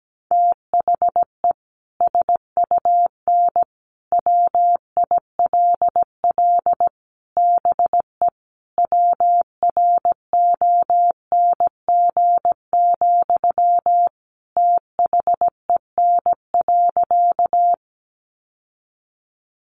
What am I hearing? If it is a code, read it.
Morse code, heard as a keyed tone, about 17 wpm, 700 Hz. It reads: THE SUN WILL BE WRONG, THEN.
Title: Sentences from Top 900 Words 17wpm